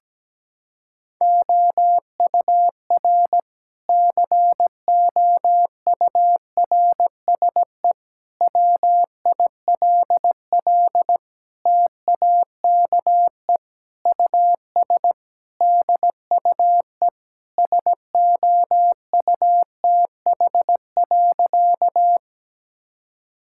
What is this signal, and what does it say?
Morse code, heard as a keyed tone, about 17 wpm, 700 Hz. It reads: OUR COURSE WILL TAKE US DUE SOUTH.
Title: Sentences from Top 1000 Words - Review Part 6 - 17wpm